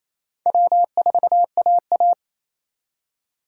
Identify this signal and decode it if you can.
Morse code, heard as a keyed tone, about 34 wpm, 700 Hz. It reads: W4AA
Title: Speed Racing - USA 1x2 Call Signs 28wpm